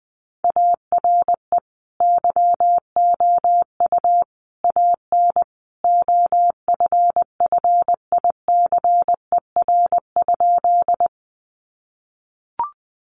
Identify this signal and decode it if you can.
Morse code, heard as a keyed tone, about 20 wpm, 700 Hz. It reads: ARE YOU AN OFFICER?
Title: Sentences from Top 1200 Words 20wpm